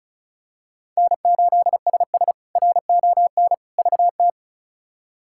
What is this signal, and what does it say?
Morse code, heard as a keyed tone, about 36 wpm, 700 Hz. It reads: N8SS RON VT
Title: K1USN SST QSO Practice - 35wpm